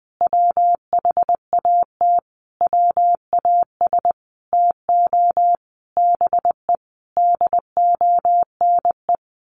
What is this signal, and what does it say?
Morse code, heard as a keyed tone, about 20 wpm, 700 Hz. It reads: WHAT WAS TO BE DONE
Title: Sentences from Top 1200 Words 20wpm